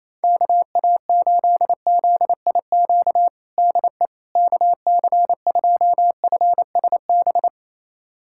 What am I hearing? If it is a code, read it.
Morse code, heard as a keyed tone, about 28 wpm, 700 Hz. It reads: KA8ZIQ DE KC2FSB
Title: USA Callsign DE Callsign 28wpm